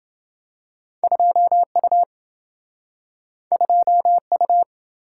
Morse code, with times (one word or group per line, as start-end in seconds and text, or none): 1.0-2.0 2U
3.5-4.6 2U